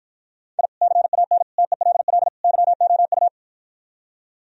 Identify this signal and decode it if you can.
Morse code, heard as a keyed tone, about 53 wpm, 700 Hz. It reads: I CAN TELL YOU